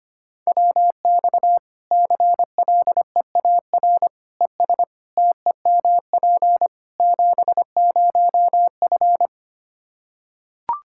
Morse code, heard as a keyed tone, about 25 wpm, 700 Hz. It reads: WX CLEAR ES TEMP 70F
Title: K1USN SST QSO Practice 25wpm